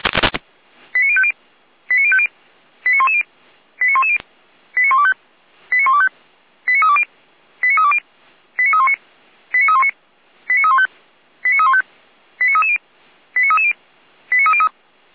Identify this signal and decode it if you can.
Five-tone selective call (ZVEI), heard as quick runs of five tones, tones of about 70 ms. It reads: 89050 89050 89110 89110 89125 89125 89320 89320 89319 89319 89315 89315 89330 89330 89303
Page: SDR/ZVEI